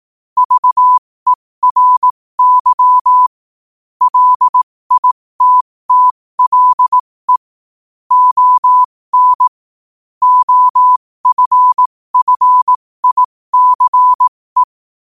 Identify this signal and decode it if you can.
Morse code, heard as a keyed tone, about 18 wpm, spaced as for 15 wpm, 1000 Hz. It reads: VERY LITTLE ON OFFICE
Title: 15 WPM morse code quotes for Fri, 01 Mar 2024